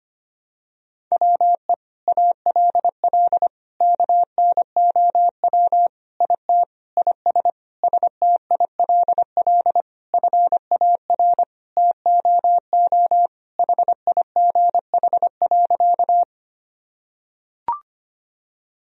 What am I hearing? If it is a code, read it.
Morse code, heard as a keyed tone, about 25 wpm, 700 Hz. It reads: WE ALL KNOW IT IS STILL FAR TOO HIGH.
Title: Sentences from Top 1000 Words - Review Part 1 - 25wpm